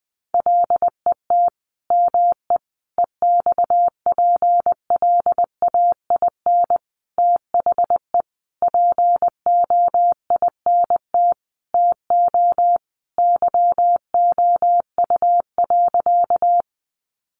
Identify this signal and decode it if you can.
Morse code, heard as a keyed tone, about 20 wpm, 700 Hz. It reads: LET ME EXPLAIN THE POINT TO YOU.
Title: Sentences from Top 1100 Words 20wpm